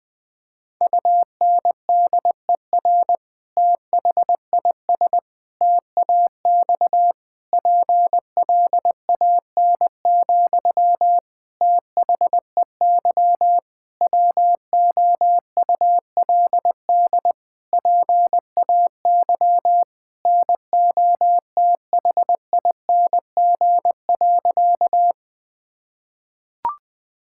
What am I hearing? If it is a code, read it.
Morse code, heard as a keyed tone, about 20 wpm, 700 Hz. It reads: UNDER THIS TAX PLAN, THEY WOULD PAY NOTHING.
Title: Sentences from Top 1000 Words - Review Part 5 - 20wpm